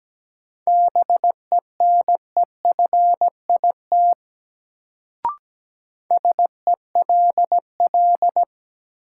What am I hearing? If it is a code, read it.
Morse code, heard as a keyed tone, about 17 wpm, 700 Hz. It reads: BENEFIT SELL
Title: Sentences from Top 1000 Words - Review Part 6 - New Word Drill - 17wpm